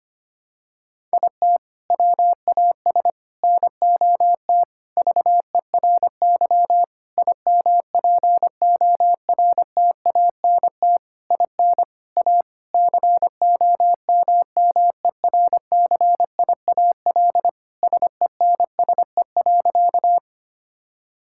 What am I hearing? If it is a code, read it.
Morse code, heard as a keyed tone, about 25 wpm, 700 Hz. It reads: IT WAS NOT VERY IMPORTANT IN A COMMERCIAL SENSE.
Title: Sentences from Top 1000 Words - Review Part 5 - 25wpm